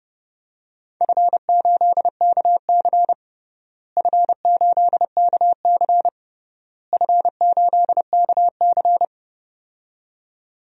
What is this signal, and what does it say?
Morse code, heard as a keyed tone, about 30 wpm, 700 Hz. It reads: F8KC F8KC F8KC
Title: DXCC-100 Call Signs - 4 Characters - Repeated 3x - 30wpm